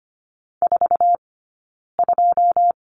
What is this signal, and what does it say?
Morse code, heard as a keyed tone, about 25 wpm, 700 Hz. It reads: 4 2
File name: course / 25:10wpm / 099 Lesson 17d - Letter u - Two Characters 25wpm